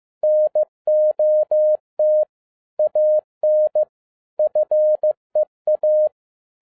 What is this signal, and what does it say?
Morse code, heard as a keyed tone, about 15 wpm, 610 Hz. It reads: NOT AN FEA
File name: 2012-01-mp3 / n17-SUMMARY OF FEA NET IN 2011-15wpm